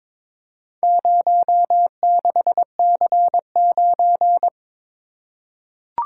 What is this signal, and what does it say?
Morse code, heard as a keyed tone, about 22 wpm, 700 Hz. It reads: 06C9
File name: practice / 22wpm / Four Letter-Number Combinations - 22wpm